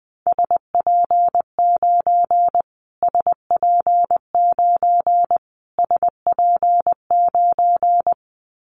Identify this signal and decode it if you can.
Morse code, heard as a keyed tone, about 20 wpm, 700 Hz. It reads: SP9 SP9 SP9
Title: Callsign Prefixes for Top 25 DXCC - Repeated 3x - 20wpm